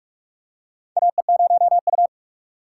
Morse code, heard as a keyed tone, about 43 wpm, 700 Hz. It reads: AE0U
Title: USA Call Signs - 4 Characters - 4x Character Spacing - 45wpm